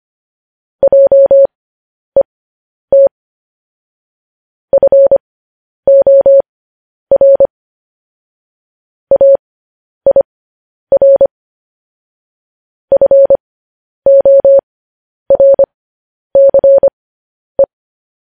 Morse code, heard as a keyed tone, about 25 wpm, spaced as for 10 wpm, 550 Hz. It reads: JET FOR AIR FORCE
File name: podcast / News headlines-1x-25x10